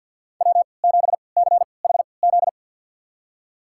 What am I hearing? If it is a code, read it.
Morse code, heard as a keyed tone, about 53 wpm, spaced as for 29 wpm, 700 Hz. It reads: W7CHZ
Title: USA Call Signs - 5 Characters - 3x Character Spacing - 50wpm